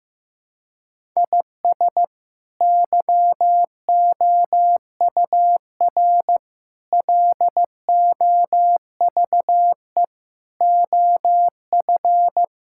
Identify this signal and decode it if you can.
Morse code, heard as a keyed tone, about 14 wpm, 700 Hz. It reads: IS YOUR LOVE OF
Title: Sentences from Top 1000 Words - Review Part 5 - 15wpm